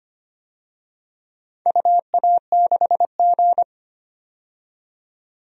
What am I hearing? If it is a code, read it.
Morse code, heard as a keyed tone, about 25 wpm, 700 Hz. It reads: UA6G
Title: DXCC-100 Call Signs - 4 Characters - 25wpm